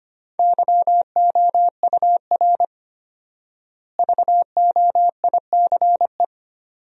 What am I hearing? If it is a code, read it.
Morse code, heard as a keyed tone, about 25 wpm, 700 Hz. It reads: YOUR VOICE
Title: 4x Word-Spacing - Sentences from Top 500 Words - 25wpm